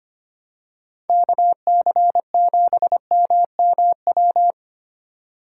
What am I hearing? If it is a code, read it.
Morse code, heard as a keyed tone, about 25 wpm, 700 Hz. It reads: KC7MMW
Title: USA Call Signs - 4x Character Spacing - 25wpm